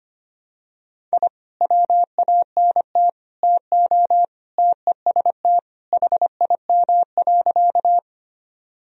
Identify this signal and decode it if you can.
Morse code, heard as a keyed tone, about 25 wpm, 700 Hz. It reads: I WANT TO TEST HIM.
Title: Sentences from Top 1000 Words - Review Part 3 - 25wpm